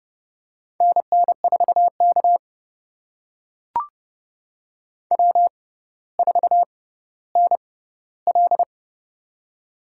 Morse code, heard as a keyed tone, about 30 wpm, spaced as for 15 wpm, 700 Hz. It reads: NN4K W4NL
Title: USA Call Signs - 4 Characters - 6x Character Spacing - 30wpm